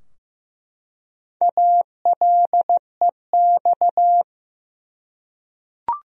Morse code, heard as a keyed tone, about 15 wpm, 700 Hz. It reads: ALEX
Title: USA Common Names - 15wpm